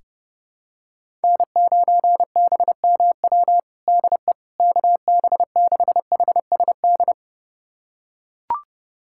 Morse code, heard as a keyed tone, about 30 wpm, 700 Hz. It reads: N9BMW DE KB6HSD
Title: USA Callsign DE Callsign 30wpm